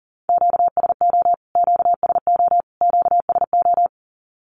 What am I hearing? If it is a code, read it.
Morse code, heard as a keyed tone, about 40 wpm, 700 Hz. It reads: QSO QSO QSO
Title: Q-Codes Repeated 3x 40wpm